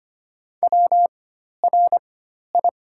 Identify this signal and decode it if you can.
Morse code, heard as a keyed tone, about 25 wpm, spaced as for 12 wpm, 700 Hz. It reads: WRI
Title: 4x Character Spacing - Top 200 Words - 25wpm